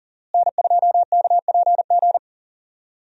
Morse code, heard as a keyed tone, about 39 wpm, 700 Hz. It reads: N2KPG
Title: USA Call Signs - 5 Characters - Repeated 2x - 40wpm